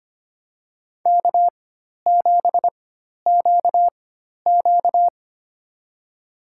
Morse code, heard as a keyed tone, about 25 wpm, spaced as for 12 wpm, 700 Hz. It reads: K7QQ
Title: USA Call Signs - 4 Characters - 4x Character Spacing - 25wpm